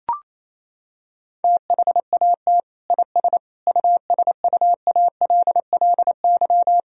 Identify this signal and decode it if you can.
Morse code, heard as a keyed tone, about 28 wpm, 700 Hz. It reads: THAT IS USUALLY
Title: Sentences from Top 1000 Words - Review Part 3 - 28wpm